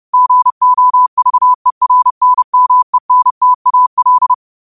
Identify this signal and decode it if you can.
Morse code, heard as a keyed tone, about 30 wpm, 1000 Hz. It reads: GOVERNMENTAL
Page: New quotes every day in morse code at 30 Words per minute.